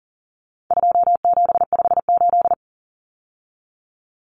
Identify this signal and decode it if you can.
Morse code, heard as a keyed tone, about 40 wpm, 700 Hz. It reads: 2758
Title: Numbers - 4-Digit 40wpm